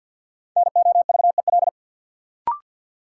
Morse code, heard as a keyed tone, about 53 wpm, 700 Hz. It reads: NOVEL
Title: Sentences from Top 1300 Words - New Word Drill Alt - 50wpm